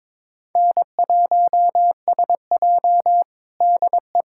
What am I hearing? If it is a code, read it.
Morse code, heard as a keyed tone, about 22 wpm, 700 Hz. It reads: N1SJ DE
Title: USA Callsign DE Callsign - 4 Characters - 22wpm